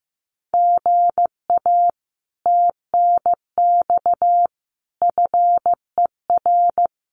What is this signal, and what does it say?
Morse code, heard as a keyed tone, about 15 wpm, 700 Hz. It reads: GA TNX FER
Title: Common QSO Phrases Encore 15wpm